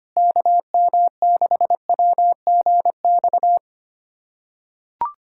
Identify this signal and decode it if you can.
Morse code, heard as a keyed tone, about 25 wpm, 700 Hz. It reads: KM6WGX
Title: USA Callsign DE Callsign 25wpm